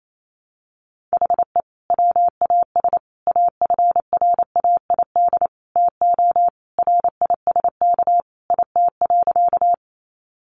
Morse code, heard as a keyed tone, about 28 wpm, 700 Hz. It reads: HE WAS AFRAID TO RISK IT.
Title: Sentences from Top 1500 Words 28wpm